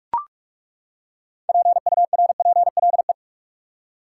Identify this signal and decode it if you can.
Morse code, heard as a keyed tone, about 41 wpm, 700 Hz. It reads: PURPLE
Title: Six Letter Words - 45wpm